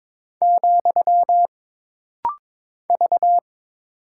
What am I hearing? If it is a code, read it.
Morse code, heard as a keyed tone, about 22 wpm, 700 Hz. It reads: , V
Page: This rapid-fire timing is geared to push your ICR (Instant Character Recognition) skills.